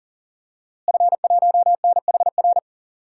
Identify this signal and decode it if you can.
Morse code, heard as a keyed tone, about 40 wpm, 700 Hz. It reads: F1NHR
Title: DXCC-100 Call Signs - 5 Characters - Repeated 2x - 40wpm